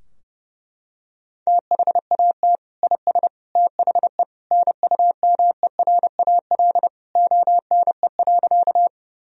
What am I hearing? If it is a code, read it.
Morse code, heard as a keyed tone, about 30 wpm, 700 Hz. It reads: THAT IS THE NUMERAL ONE.
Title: Sentences from Top 1000 Words - Review Part 2 - 30wpm